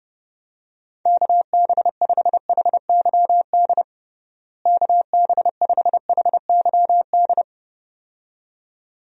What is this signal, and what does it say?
Morse code, heard as a keyed tone, about 30 wpm, 700 Hz. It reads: KB5HYD KB5HYD
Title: USA Call Signs - 6 Characters - Repeated 2x - 30wpm